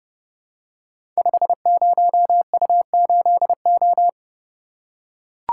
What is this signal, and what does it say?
Morse code, heard as a keyed tone, about 30 wpm, 700 Hz. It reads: 50U8O
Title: Code Groups - 5 Letter:Numbers - 30wpm